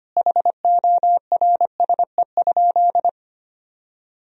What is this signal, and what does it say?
Morse code, heard as a keyed tone, about 25 wpm, 700 Hz. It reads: HORSE?
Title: Sentences from Top 1000 Words - Review Part 1 - 25wpm